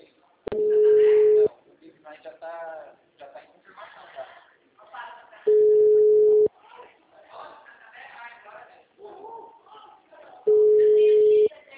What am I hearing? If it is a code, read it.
Ringback tone (Europe, 425 Hz), with speech.